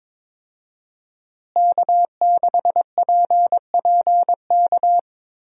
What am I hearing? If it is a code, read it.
Morse code, heard as a keyed tone, about 22 wpm, 700 Hz. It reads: K6PPK
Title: USA Call Signs - 5 Characters - Repeated 3x - 22wpm